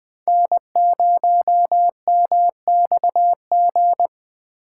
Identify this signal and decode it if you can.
Morse code, heard as a keyed tone, about 20 wpm, 700 Hz. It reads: N0MXG
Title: USA Call Signs - 4x Character Spacing - 20wpm